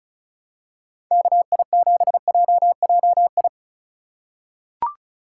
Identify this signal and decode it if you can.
Morse code, heard as a keyed tone, about 34 wpm, 700 Hz. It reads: KI7JJI
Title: USA Call Signs - 6x Character Spacing - 35wpm